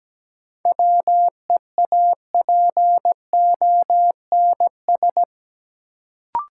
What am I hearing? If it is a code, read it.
Morse code, heard as a keyed tone, about 17 wpm, 700 Hz. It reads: WEAPONS
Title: Sets of 5 Words Encore - New Word Drill - 17wpm